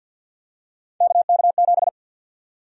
Morse code, heard as a keyed tone, about 48 wpm, 700 Hz. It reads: KK7
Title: USA Call Sign Prefixes - Repeated 2x - 50wpm